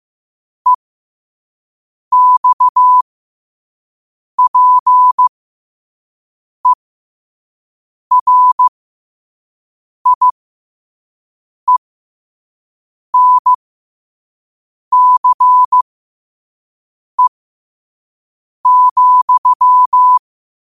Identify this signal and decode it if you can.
Morse code, heard as a keyed tone, about 15 wpm, spaced as for 5 wpm, 1000 Hz. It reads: EXPERIENCE,
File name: qotdfiles / 5wpm